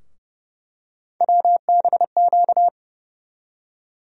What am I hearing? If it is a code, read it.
Morse code, heard as a keyed tone, about 29 wpm, 700 Hz. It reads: WBQ
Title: USA Call Sign Postfixes - 3 Letters 30wpm